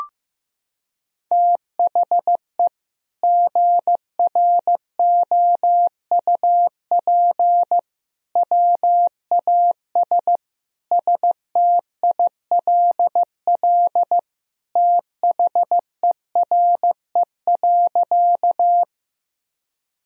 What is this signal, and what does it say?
Morse code, heard as a keyed tone, about 15 wpm, 700 Hz. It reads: THE GROUP WAS STILL THERE.
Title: Sentences from Top 1000 Words - Review Part 1 - 15wpm